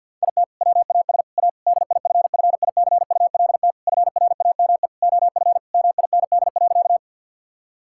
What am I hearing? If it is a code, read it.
Morse code, heard as a keyed tone, about 50 wpm, 700 Hz. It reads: IT WAS A DIFFICULT FRAME OF MIND.
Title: Sentences from Top 1000 Words - Review Part 6 - 50wpm